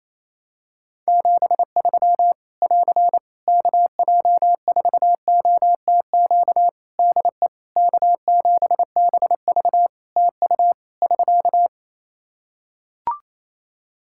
Morse code, heard as a keyed tone, about 28 wpm, 700 Hz. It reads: 73 <AR> KJ4OTQ DE K7BV TU <SK>
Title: K1USN SST QSO Practice 28wpm